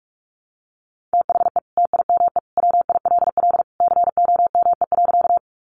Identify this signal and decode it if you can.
Morse code, heard as a keyed tone, about 44 wpm, 700 Hz. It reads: THE TIME WILL COME.
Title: Sentences from Top 100 Words 45wpm